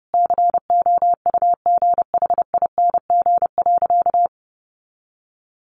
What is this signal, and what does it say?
Morse code, heard as a keyed tone, about 30 wpm, 700 Hz. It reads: COUGHING.
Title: Additional Easy Phrases 30wpm